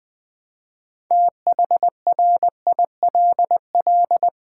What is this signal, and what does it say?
Morse code, heard as a keyed tone, about 20 wpm, 700 Hz. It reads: THRILL
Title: Build A Word - 3 Letters At A Time - 8-Letter Words - 20wpm